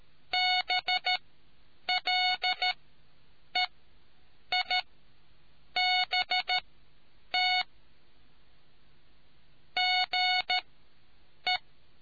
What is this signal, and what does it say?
Morse code, heard as a keyed tone, about 13 wpm, spaced as for 7 wpm, 730 Hz. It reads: BLEIBT GE